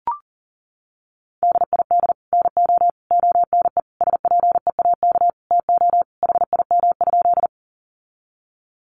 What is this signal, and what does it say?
Morse code, heard as a keyed tone, about 40 wpm, 700 Hz. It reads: DID NO ONE SPEAK TO HIM?
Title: Sentences from Top 1100 Words 40wpm